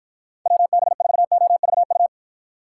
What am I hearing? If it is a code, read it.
Morse code, heard as a keyed tone, about 53 wpm, 700 Hz. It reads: WB4OVU